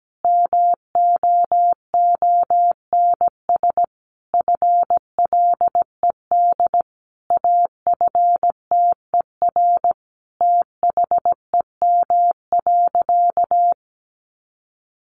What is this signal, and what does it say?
Morse code, heard as a keyed tone, about 17 wpm, 700 Hz. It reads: MOONS FLED AFTER THEM.
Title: A Sound Of Thunder 17wpm